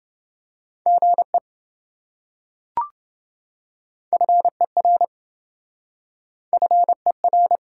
Morse code, heard as a keyed tone, about 26 wpm, 700 Hz. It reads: GE FER FER